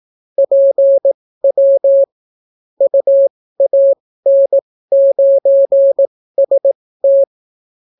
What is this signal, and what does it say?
Morse code, heard as a keyed tone, about 18 wpm, spaced as for 14 wpm, 550 Hz. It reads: PW UAN9ST